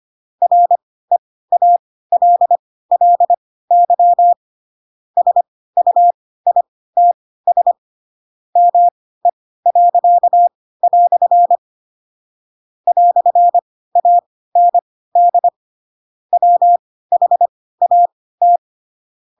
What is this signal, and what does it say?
Morse code, heard as a keyed tone, about 25 wpm, spaced as for 16 wpm, 700 Hz. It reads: REALLY SUITS ME." "AND WHAT
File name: Pinocchio / Pinocchio ch4 10 25 ik1hge